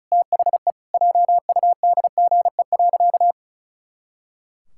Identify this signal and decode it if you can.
Morse code, heard as a keyed tone, about 34 wpm, 700 Hz. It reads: THE JUDGE.
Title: Pangram Challenge v2 - 35wpm